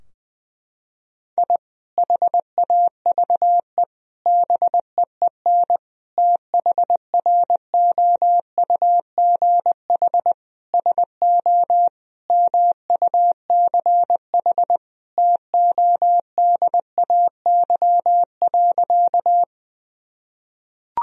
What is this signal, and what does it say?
Morse code, heard as a keyed tone, about 20 wpm, 700 Hz. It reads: I HAVE BEEN THROUGH SO MUCH TODAY.
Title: Sentences from Top 1000 Words - Review Part 4 - 20wpm